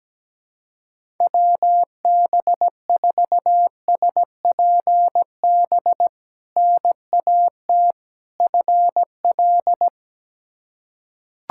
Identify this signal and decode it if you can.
Morse code, heard as a keyed tone, about 17 wpm, 700 Hz. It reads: WB4SPB NAT FL
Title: K1USN SST QSO Practice - 17wpm